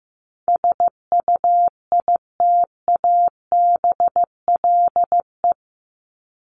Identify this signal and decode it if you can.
Morse code, heard as a keyed tone, about 15 wpm, 700 Hz. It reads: SUITABLE
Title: Top 1251-1500 Words 15wpm